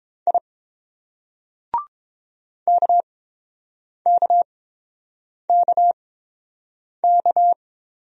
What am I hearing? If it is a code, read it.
Morse code, heard as a keyed tone, about 30 wpm, spaced as for 7 wpm, 700 Hz. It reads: I KKKK